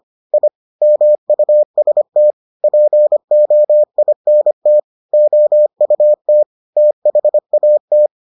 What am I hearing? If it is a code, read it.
Morse code, heard as a keyed tone, about 25 wpm, 600 Hz. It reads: I MUST POINT OUT THAT